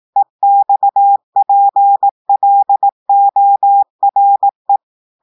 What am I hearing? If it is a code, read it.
Morse code, heard as a keyed tone, about 18 wpm, 800 Hz. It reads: EXPLORE
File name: audios / portugais / portugais texte en clair 2 vitesse 18 mots minutes